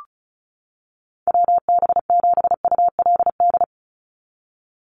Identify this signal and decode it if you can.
Morse code, heard as a keyed tone, about 34 wpm, 700 Hz. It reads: WB7ULD
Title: NA SOTA Chaser Callsigns 35wpm